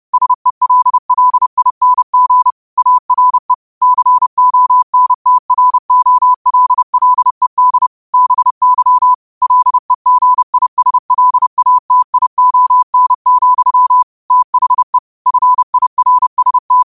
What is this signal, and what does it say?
Morse code, heard as a keyed tone, about 30 wpm, 1000 Hz. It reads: SELLING ARE CONTROLLED BY LEGISLATION, THE FIRST
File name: qotdfiles / 30wpm